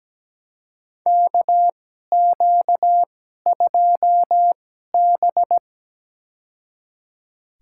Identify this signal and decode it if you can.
Morse code, heard as a keyed tone, about 17 wpm, spaced as for 12 wpm, 700 Hz. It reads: KQ2B
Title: USA Call Signs - 4 Characters - 2x Character Spacing - 17wpm